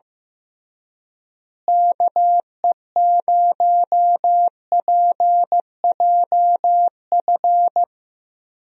Morse code, heard as a keyed tone, about 16 wpm, 700 Hz. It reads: KE0PJF
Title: USA Call Signs - 6 Characters - Repeated 3x - 15wpm